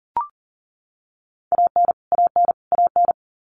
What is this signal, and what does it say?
Morse code, heard as a keyed tone, about 40 wpm, 700 Hz. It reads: AN AN AN
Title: Top 500 Words Repeated 40wpm